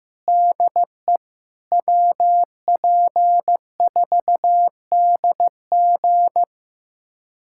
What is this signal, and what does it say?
Morse code, heard as a keyed tone, about 16 wpm, 700 Hz. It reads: DE WP4DG
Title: USA Callsign DE Callsign - 5 Characters - 15wpm